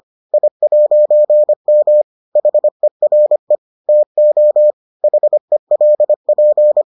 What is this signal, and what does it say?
Morse code, heard as a keyed tone, about 25 wpm, 600 Hz. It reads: I'M HERE TO HELP